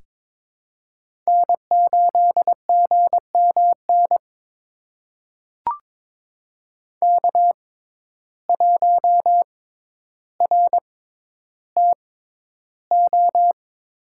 Morse code, heard as a keyed tone, about 22 wpm, spaced as for 11 wpm, 700 Hz. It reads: N8GMN K1RTO
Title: USA Call Signs - 6x Character Spacing - 22wpm